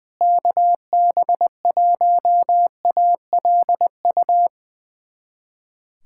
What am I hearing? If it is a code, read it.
Morse code, heard as a keyed tone, about 20 wpm, 700 Hz. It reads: KB1ALU
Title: USA Callsign DE Callsign 20wpm